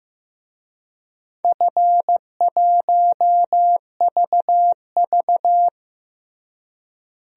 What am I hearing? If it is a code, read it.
Morse code, heard as a keyed tone, about 15 wpm, 700 Hz. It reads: F1VV
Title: DXCC-100 Call Signs - 4 Characters - Repeated 3x - 15wpm